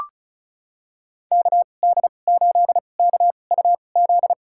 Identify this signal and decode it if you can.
Morse code, heard as a keyed tone, about 36 wpm, spaced as for 25 wpm, 700 Hz. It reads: KD8KUZ
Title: USA Call Signs - 6 Characters - 2x Character Spacing - 35wpm